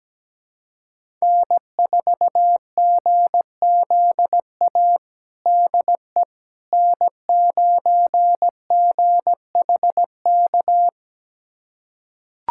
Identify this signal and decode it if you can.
Morse code, heard as a keyed tone, about 17 wpm, 700 Hz. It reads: N4GZA DE N9GHK
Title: USA Callsign DE Callsign - 5 Characters - 17wpm